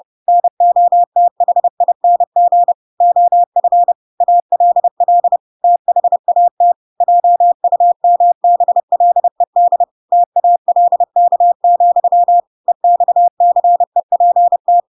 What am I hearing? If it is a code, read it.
Morse code, heard as a keyed tone, about 30 wpm, 700 Hz. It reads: NOTHING OF ALL THAT JUMBLED TALK, EXCEPT
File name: Pinocchio / Pinocchio ch7 30 30 ik1hge